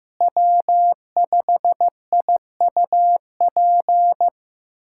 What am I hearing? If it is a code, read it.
Morse code, heard as a keyed tone, about 15 wpm, 700 Hz. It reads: W5IUP
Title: USA Callsign DE Callsign - 5 Characters - 15wpm